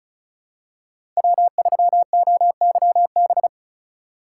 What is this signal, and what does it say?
Morse code, heard as a keyed tone, about 36 wpm, 700 Hz. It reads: W3OYB
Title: USA Call Signs - 5 Characters - 6x Character Spacing - 35wpm